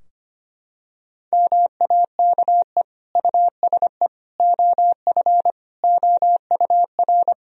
Morse code, heard as a keyed tone, about 25 wpm, 700 Hz. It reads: MAKE USE OF OUR
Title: Sentences from Top 1000 Words - Review Part 4 - 25wpm